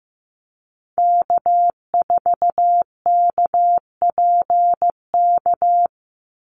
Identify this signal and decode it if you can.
Morse code, heard as a keyed tone, about 15 wpm, 700 Hz. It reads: K4KPK
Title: NA SOTA Activator Callsigns 15wpm